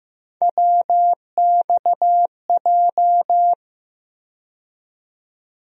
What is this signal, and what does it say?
Morse code, heard as a keyed tone, about 15 wpm, 700 Hz. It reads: WXJ
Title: USA Call Sign Postfixes - 3 Letters 15wpm